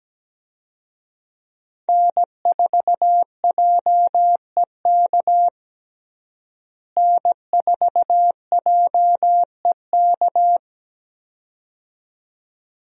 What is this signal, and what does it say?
Morse code, heard as a keyed tone, about 17 wpm, 700 Hz. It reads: N4JEK N4JEK
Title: USA Call Signs - 5 Characters - Repeated 2x - 17wpm